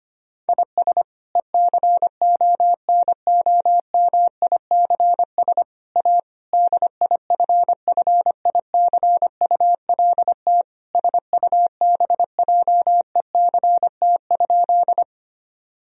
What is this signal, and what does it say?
Morse code, heard as a keyed tone, about 25 wpm, 700 Hz. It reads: IS ECONOMICS A DIFFICULT SUBJECT?
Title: Sentences from Top 1000 Words - Review Part 3 - 25wpm